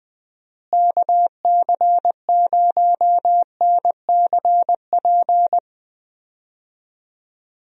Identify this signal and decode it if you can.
Morse code, heard as a keyed tone, about 21 wpm, 700 Hz. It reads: KC0NCP
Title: USA Call Signs - 6 Characters - Repeated 3x - 20wpm